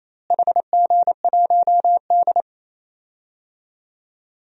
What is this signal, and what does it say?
Morse code, heard as a keyed tone, about 28 wpm, 700 Hz. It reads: HG1D
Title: DXCC-100 Call Signs - 4 Characters - Repeated 2x - 28wpm